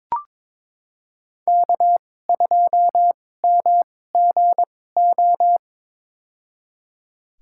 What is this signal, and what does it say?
Morse code, heard as a keyed tone, about 22 wpm, spaced as for 16 wpm, 700 Hz. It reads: K2MGO
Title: USA Call Signs - 5 Characters - 2x Character Spacing - 22wpm